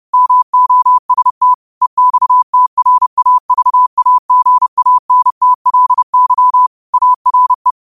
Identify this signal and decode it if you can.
Morse code, heard as a keyed tone, about 30 wpm, 1000 Hz. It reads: MOST EXTRAVAGANTLY ARE